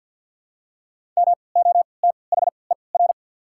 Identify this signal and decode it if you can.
Morse code, heard as a keyed tone, about 47 wpm, 700 Hz. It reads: M O T H E R
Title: 3x Character Spacing - Top 200 Words - 50wpm